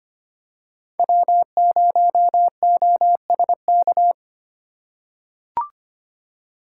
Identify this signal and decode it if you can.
Morse code, heard as a keyed tone, about 25 wpm, 700 Hz. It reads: W0OSK
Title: USA Call Signs - 4x Character Spacing - 25wpm